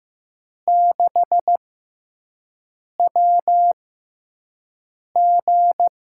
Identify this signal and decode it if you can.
Morse code, heard as a keyed tone, about 15 wpm, spaced as for 5 wpm, 700 Hz. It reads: 6WG
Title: USA Call Signs - 4 Characters - 6x Character Spacing - 15wpm